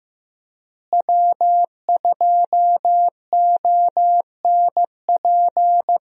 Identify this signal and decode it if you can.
Morse code, heard as a keyed tone, about 15 wpm, 700 Hz. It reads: W2ONP
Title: USA Call Signs - 5 Characters - Repeated 3x - 15wpm